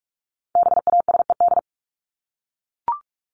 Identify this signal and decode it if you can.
Morse code, heard as a keyed tone, about 44 wpm, 700 Hz. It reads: BASED
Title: ED Suffix - Top 100 Words with Base Words 45wpm